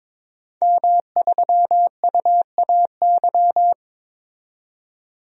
Morse code, heard as a keyed tone, about 21 wpm, 700 Hz. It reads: M3UAY
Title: DXCC-100 Call Signs - 5 Characters - 22wpm